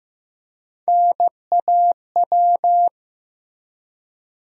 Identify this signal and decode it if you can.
Morse code, heard as a keyed tone, about 15 wpm, 700 Hz. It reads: NAW
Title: USA Call Sign Postfixes - 3 Letters 15wpm